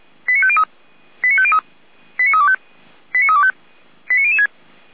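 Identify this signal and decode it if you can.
Five-tone selective call (ZVEI), heard as quick runs of five tones, tones of about 70 ms. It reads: 89503 89503 89325 89325 89006
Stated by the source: ZVEI five-tone selective call received on an SDR.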